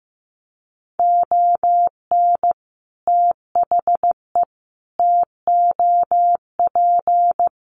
Morse code, heard as a keyed tone, about 15 wpm, 700 Hz. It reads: ON THE TOP
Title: Sets of 5 Words Encore 15wpm